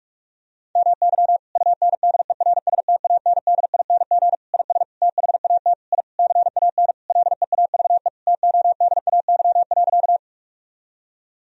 Morse code, heard as a keyed tone, about 42 wpm, 700 Hz. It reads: MY UNDERSTANDING IS THAT I CAN LEAVE TODAY.
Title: Sentences from Top 1000 Words - Review Part 3 - 45wpm